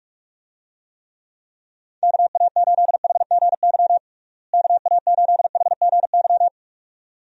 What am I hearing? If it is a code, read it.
Morse code, heard as a keyed tone, about 46 wpm, 700 Hz. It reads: KA8HGY KA8HGY
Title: USA Call Signs - 6 Characters - Repeated 2x - 45wpm